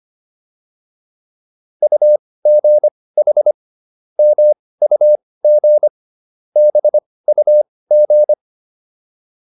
Morse code, heard as a keyed tone, about 25 wpm, spaced as for 18 wpm, 600 Hz. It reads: UGH MUG BUG
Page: intermediate-practice-files